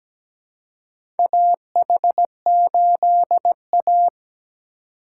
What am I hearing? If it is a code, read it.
Morse code, heard as a keyed tone, about 17 wpm, 700 Hz. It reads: AH8A
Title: USA Call Signs - 4 Characters - 4x Character Spacing - 17wpm